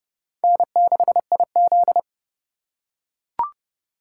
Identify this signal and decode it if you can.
Morse code, heard as a keyed tone, about 31 wpm, 700 Hz. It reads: N6IZ